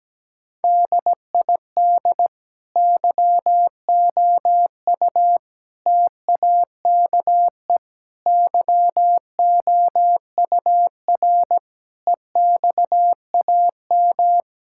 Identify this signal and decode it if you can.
Morse code, heard as a keyed tone, about 17 wpm, 700 Hz. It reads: DID YOU TAKE YOUR EXAM
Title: Sentences from Top 1000 Words - Review Part 5 - 17wpm